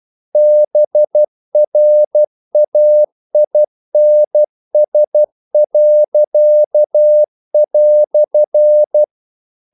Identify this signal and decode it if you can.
Morse code, heard as a keyed tone, about 12 wpm, 600 Hz. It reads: BRAINS."
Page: English quotations in CW (telegraphy - morse)